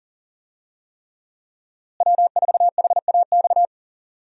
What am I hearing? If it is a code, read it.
Morse code, heard as a keyed tone, about 40 wpm, 700 Hz. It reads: W4HAX
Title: USA Call Signs - 5 Characters - Repeated 3x - 40wpm